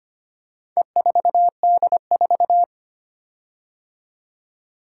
Morse code, heard as a keyed tone, about 25 wpm, 700 Hz. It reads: E4D4
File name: practice / 25wpm / Four Letter-Number Combinations - 25wpm